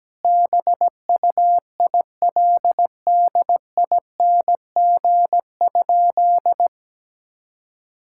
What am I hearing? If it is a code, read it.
Morse code, heard as a keyed tone, about 17 wpm, 700 Hz. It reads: BUILDING?
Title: Sentences from Top 1000 Words - Review Part 5 - 17wpm